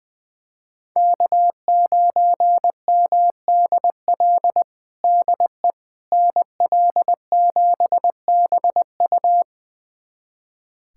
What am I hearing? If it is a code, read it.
Morse code, heard as a keyed tone, about 21 wpm, 700 Hz. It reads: K9MDL DE NL7BU
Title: USA Callsign DE Callsign - 5 Characters - 20wpm